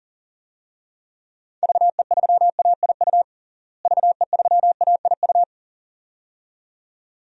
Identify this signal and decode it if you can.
Morse code, heard as a keyed tone, about 40 wpm, 700 Hz. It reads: VE3AIU VE3AIU
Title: DXCC-100 Call Signs - 6 Characters - Repeated 2x - 40wpm